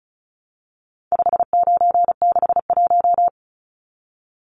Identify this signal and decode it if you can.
Morse code, heard as a keyed tone, about 34 wpm, 700 Hz. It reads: 5961
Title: Numbers - 4-Digit 35wpm